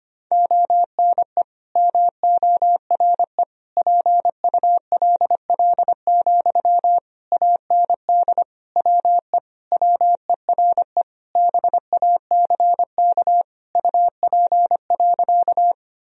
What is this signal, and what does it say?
Morse code, heard as a keyed tone, about 25 wpm, 700 Hz. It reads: ONE MORE PULL, AND WE WERE BACK UP.
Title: Sentences from Top 1000 Words - Review Part 2 - 25wpm